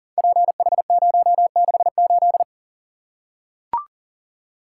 Morse code, heard as a keyed tone, about 40 wpm, 700 Hz. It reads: PH068
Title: Code Groups - 5 Letter:Numbers - 40wpm